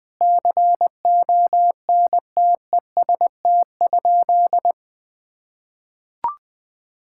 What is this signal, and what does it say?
Morse code, heard as a keyed tone, about 20 wpm, 700 Hz. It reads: CONTEST?
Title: Sentences from Top 1600 Words 20wpm